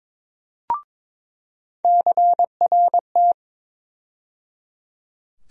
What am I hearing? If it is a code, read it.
Morse code, heard as a keyed tone, about 22 wpm, 700 Hz. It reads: CRT
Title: USA Call Sign Postfixes - 3 Letters 22wpm